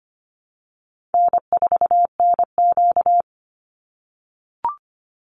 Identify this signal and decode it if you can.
Morse code, heard as a keyed tone, about 25 wpm, 700 Hz. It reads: N4NQ
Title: Easy Call Signs - 25wpm